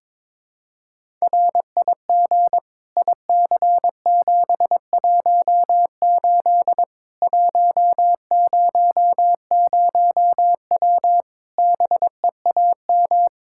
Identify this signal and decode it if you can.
Morse code, heard as a keyed tone, about 22 wpm, 700 Hz. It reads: RIG IC718 100W BEAM
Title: K1USN SST QSO Practice 22wpm